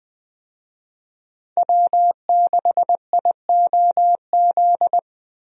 Morse code, heard as a keyed tone, about 20 wpm, 700 Hz. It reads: W6IOZ
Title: USA Call Signs - 5 Characters - Repeated 2x - 20wpm